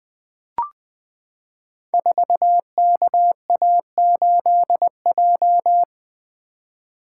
Morse code, heard as a keyed tone, about 20 wpm, 700 Hz. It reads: 4KA8J
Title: Code Groups - 5 Letter:Numbers - 20wpm